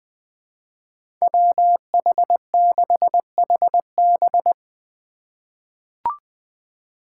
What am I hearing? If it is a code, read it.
Morse code, heard as a keyed tone, about 20 wpm, 700 Hz. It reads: WH6HB
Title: USA Call Signs - 2x Character Spacing - 20wpm